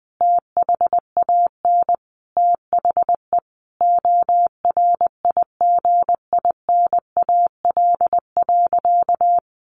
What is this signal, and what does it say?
Morse code, heard as a keyed tone, about 20 wpm, 700 Hz. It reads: THAN THE ORIGINAL.
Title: Sentences from Top 1400 Words 20wpm